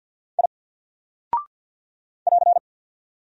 Morse code, heard as a keyed tone, about 50 wpm, 700 Hz. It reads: I <AR>
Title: Single Character (Letter:Number:Punctuation:Prosign) 50wpm